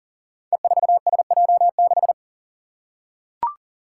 Morse code, heard as a keyed tone, about 38 wpm, 700 Hz. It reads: E4SJ6
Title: Five Challenging Letter-Number Combinations 40wpm